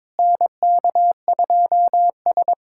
Morse code, heard as a keyed tone, about 21 wpm, 700 Hz. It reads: NK2S
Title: USA Callsign DE Callsign - 4 Characters - 22wpm